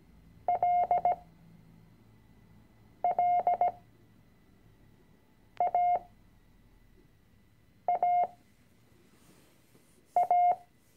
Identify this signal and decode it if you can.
Morse code, heard as a keyed tone, about 17 wpm, spaced as for 4 wpm, 710 Hz. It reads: LLAAA
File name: CW-Online-Kurs / Anfänger / Lektion 006